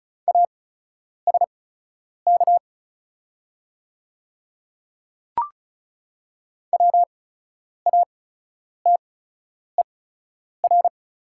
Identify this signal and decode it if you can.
Morse code, heard as a keyed tone, about 36 wpm, spaced as for 10 wpm, 700 Hz. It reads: ASK WATER
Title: 8x Character Spacing - Top 200 Words - 35wpm